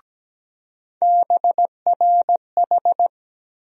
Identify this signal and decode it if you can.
Morse code, heard as a keyed tone, about 17 wpm, 700 Hz. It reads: BRH
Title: USA Call Sign Postfixes - 3 Letters 17wpm